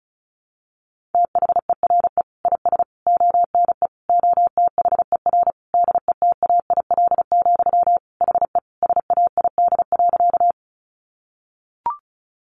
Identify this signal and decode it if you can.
Morse code, heard as a keyed tone, about 34 wpm, 700 Hz. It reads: THERE IS ONE OTHER DETAIL, HE SAID.
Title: Sentences from Top 1500 Words 35wpm